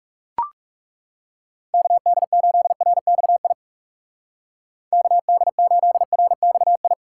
Morse code, heard as a keyed tone, about 37 wpm, 700 Hz. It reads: KD8RXI KD8RXI
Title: Speed Racing - USA 2x3 Call Signs 30wpm